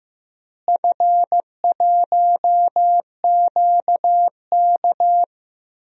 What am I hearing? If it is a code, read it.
Morse code, heard as a keyed tone, about 15 wpm, 700 Hz. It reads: F1QK
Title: DXCC-100 Call Signs - 4 Characters - Repeated 2x - 15wpm